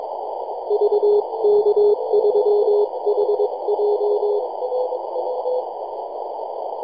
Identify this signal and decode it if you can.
Morse code, heard as a keyed tone, about 22 wpm, 420 Hz, with noise. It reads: VK3HJ
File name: yj0nk-18-075mhz-cw